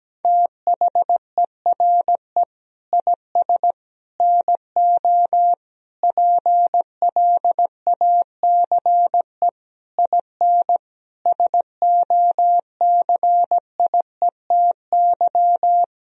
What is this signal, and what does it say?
Morse code, heard as a keyed tone, about 17 wpm, 700 Hz. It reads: THERE IS NO PLACE IN SOCIETY
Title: Sentences from Top 1000 Words - Review Part 4 - 17wpm